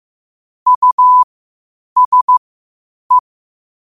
Morse code, heard as a keyed tone, about 15 wpm, spaced as for 8 wpm, 1000 Hz. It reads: USE
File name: qotdfiles / 8wpm